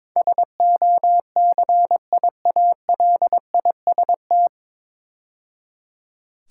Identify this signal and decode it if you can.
Morse code, heard as a keyed tone, about 22 wpm, 700 Hz. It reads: SOCIALIST
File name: 9 Letter Words - First 3 Letters Given - 22wpm